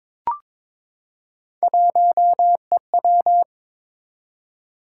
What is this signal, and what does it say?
Morse code, heard as a keyed tone, about 22 wpm, 700 Hz. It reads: 1EW
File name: practice / 22wpm / Three Letter-Number Combinations - 22wpm